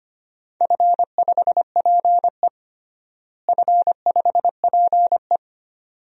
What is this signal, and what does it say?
Morse code, heard as a keyed tone, about 25 wpm, 700 Hz. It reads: F5PE F5PE
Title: DXCC-100 Call Signs - 4 Characters - Repeated 2x - 25wpm